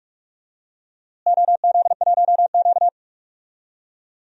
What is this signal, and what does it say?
Morse code, heard as a keyed tone, about 42 wpm, 700 Hz. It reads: OZ1Q
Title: DXCC-100 Call Signs - 4 Characters - Repeated 3x - 45wpm